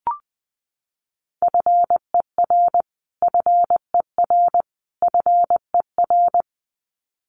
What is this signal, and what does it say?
Morse code, heard as a keyed tone, about 20 wpm, 700 Hz. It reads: FER FER FER
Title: Abbreviations Repeated 3x - 20wpm